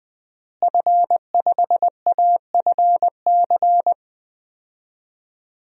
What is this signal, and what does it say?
Morse code, heard as a keyed tone, about 20 wpm, 700 Hz. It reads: F5AFC
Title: DXCC-100 Call Signs - 5 Characters - Repeated 3x - 20wpm